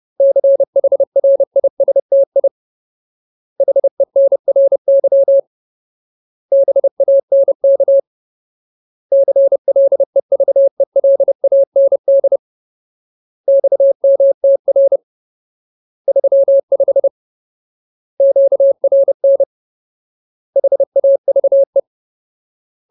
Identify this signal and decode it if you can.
Morse code, heard as a keyed tone, about 30 wpm, 550 Hz. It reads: CHRISTI HENRY BANK CLEVELAND XMTR 35 QRN HAVE
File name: Ultimate Word List / 30WPM Ultimate